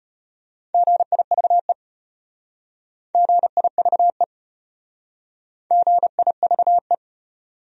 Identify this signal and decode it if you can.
Morse code, heard as a keyed tone, about 33 wpm, 700 Hz. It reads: GIVE GIVE GIVE